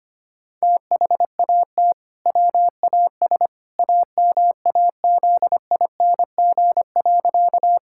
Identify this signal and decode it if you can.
Morse code, heard as a keyed tone, about 25 wpm, 700 Hz. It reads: THAT WAS AMAZING.
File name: practice / 25wpm / Sentences with 2-Word Phrases - 25wpm